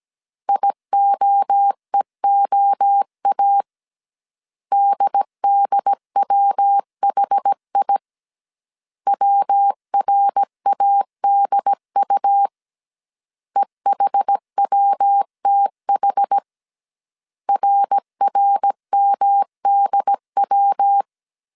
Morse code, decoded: IOEOA DDWHI WRADU EHWTH RRMDW

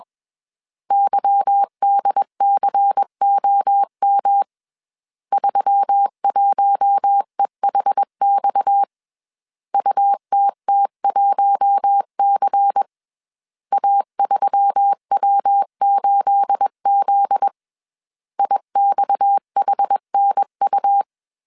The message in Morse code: YDCOM 31EHX UTT1C A3W8Z IXHNU

